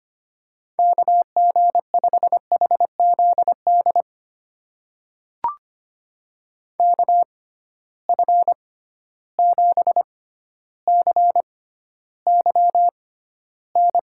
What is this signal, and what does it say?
Morse code, heard as a keyed tone, about 25 wpm, spaced as for 13 wpm, 700 Hz. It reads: KG5HZD KF7CYN
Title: USA Call Signs - 6 Characters - 6x Character Spacing - 25wpm